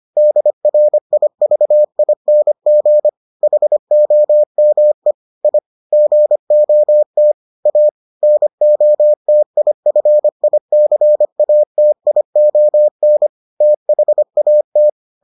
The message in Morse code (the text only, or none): DRIVING HOME I GOT A NOTIFICATION THAT